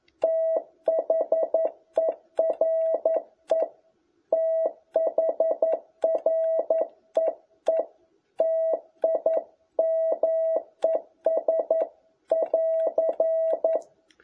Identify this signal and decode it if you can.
Morse code, heard as a keyed tone, about 11 wpm, 650 Hz. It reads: THERE THREE TIMES <AR>
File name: jh8cbh / 2021cw